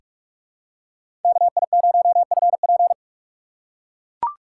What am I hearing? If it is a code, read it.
Morse code, heard as a keyed tone, about 44 wpm, 700 Hz. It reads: KI0FP
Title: USA Call Signs - 5 Characters - 6x Character Spacing - 45wpm